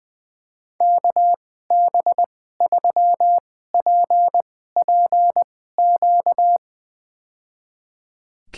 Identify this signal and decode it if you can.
Morse code, heard as a keyed tone, about 20 wpm, spaced as for 14 wpm, 700 Hz. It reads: KB3PPQ
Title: USA Call Signs - 2x Character Spacing - 20wpm